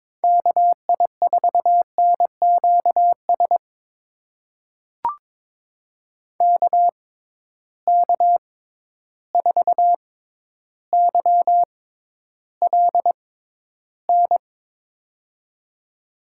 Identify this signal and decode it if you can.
Morse code, heard as a keyed tone, about 22 wpm, spaced as for 11 wpm, 700 Hz. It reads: KI4NQS KK4YLN
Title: USA Call Signs - 6x Character Spacing - 22wpm